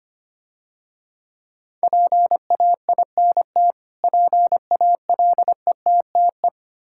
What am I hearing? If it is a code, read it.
Morse code, heard as a keyed tone, about 27 wpm, 700 Hz. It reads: PAINT PALETTE
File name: practice / 25wpm / Difficult Words - Speed-Racing with Hints - 25wpm